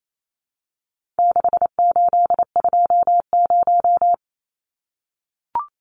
Morse code, heard as a keyed tone, about 28 wpm, 700 Hz. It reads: 6820
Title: Numbers - 4-Digit 28wpm